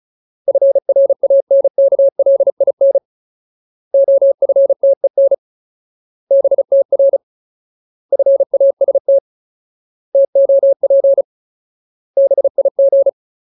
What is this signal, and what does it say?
Morse code, heard as a keyed tone, about 34 wpm, 550 Hz. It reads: FRANKLIN OFTEN BTR FAST TOP BIG
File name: Ultimate Word List / 35WPM Ultimate